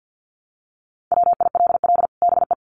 Morse code, heard as a keyed tone, about 50 wpm, 700 Hz. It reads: WILL BE
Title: Sets of 2 Words 50wpm